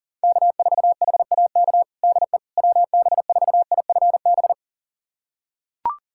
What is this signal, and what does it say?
Morse code, heard as a keyed tone, about 40 wpm, 700 Hz. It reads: K4HAK DE WB4IFB
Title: USA Callsign DE Callsign 40wpm